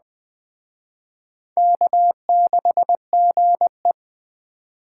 Morse code, heard as a keyed tone, about 20 wpm, 700 Hz. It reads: K6GE